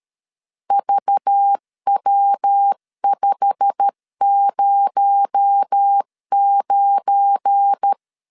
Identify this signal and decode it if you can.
Morse code, heard as a keyed tone, about 13 wpm, 800 Hz. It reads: VW509
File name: LEZIONI / 8 YJ45 / 8 YJ45 12wpm 2151